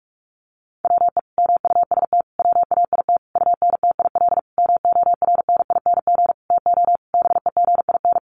Morse code, heard as a keyed tone, about 44 wpm, 700 Hz. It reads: WE MUST WAIT UNTIL MORNING TO BEGIN
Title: Sentences from Top 1300 Words 45wpm